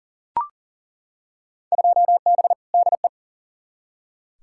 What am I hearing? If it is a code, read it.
Morse code, heard as a keyed tone, about 40 wpm, 700 Hz. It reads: 2B DE
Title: Field Day Exchanges Encore - 40wpm